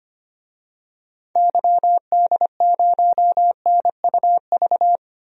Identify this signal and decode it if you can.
Morse code, heard as a keyed tone, about 25 wpm, 700 Hz. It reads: YD0NUV
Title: DXCC-100 Call Signs - 6 Characters - 25wpm